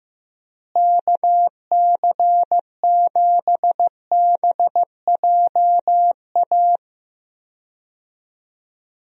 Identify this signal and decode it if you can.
Morse code, heard as a keyed tone, about 15 wpm, 700 Hz. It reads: KC7BJA
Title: USA Call Signs - 6 Characters - Repeated 3x - 15wpm